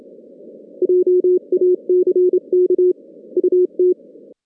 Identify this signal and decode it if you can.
Morse code, heard as a keyed tone, about 25 wpm, 360 Hz. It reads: JACK UT
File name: 2016 JAN CW NAQP / 2016 Jan naqp k5ot - 0-8